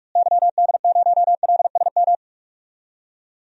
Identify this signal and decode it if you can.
Morse code, heard as a keyed tone, about 41 wpm, 700 Hz. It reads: YD0LSM
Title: DXCC-100 Call Signs - 6 Characters - Repeated 3x - 45wpm